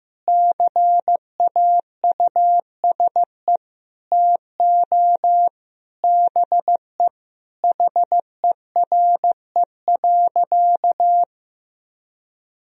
Morse code, heard as a keyed tone, about 15 wpm, 700 Hz. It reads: CAUSE TO BE HERE.
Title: Sentences from Top 1000 Words - Review Part 1 - 15wpm